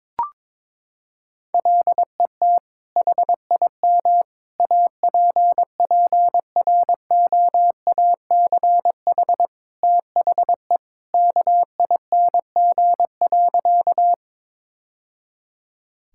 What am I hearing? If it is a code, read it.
Morse code, heard as a keyed tone, about 22 wpm, 700 Hz. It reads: LET HIM APPROACH THE KING.
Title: Sentences from Top 1000 Words - Review Part 6 - 22wpm